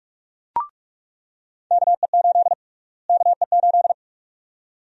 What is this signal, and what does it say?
Morse code, heard as a keyed tone, about 46 wpm, 700 Hz. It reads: KE8 KE8
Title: USA Call Sign Prefixes - Repeated 2x - 45wpm